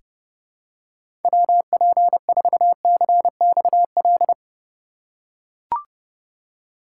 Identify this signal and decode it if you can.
Morse code, heard as a keyed tone, about 29 wpm, 700 Hz. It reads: WP4CXL
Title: USA Call Signs - 6 Characters - 4x Character Spacing - 30wpm